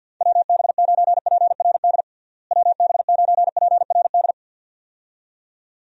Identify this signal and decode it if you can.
Morse code, heard as a keyed tone, about 50 wpm, 700 Hz. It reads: WB9PRD WB9PRD
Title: USA Call Signs - 6 Characters - Repeated 2x - 50wpm